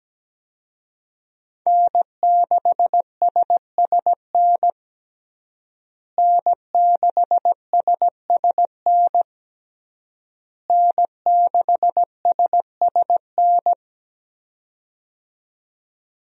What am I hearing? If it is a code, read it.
Morse code, heard as a keyed tone, about 17 wpm, 700 Hz. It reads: N6SSN N6SSN N6SSN
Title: USA Call Signs - 5 Characters - Repeated 3x - 17wpm